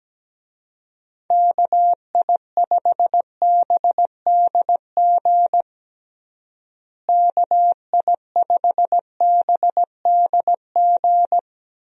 Morse code, heard as a keyed tone, about 17 wpm, 700 Hz. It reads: KI5BDG KI5BDG
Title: USA Call Signs - 6 Characters - Repeated 2x - 17wpm